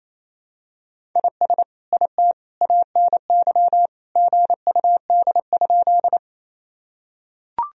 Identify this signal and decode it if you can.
Morse code, heard as a keyed tone, about 28 wpm, 700 Hz. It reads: IS IT ANY GUD?
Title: Sentences from Top 200 Words and Abbreviations 28wpm